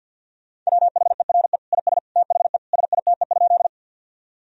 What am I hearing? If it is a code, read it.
Morse code, heard as a keyed tone, about 50 wpm, 700 Hz. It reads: WHERE IS THE SITE?
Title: Sentences from Top 1000 Words - Review Part 4 - 50wpm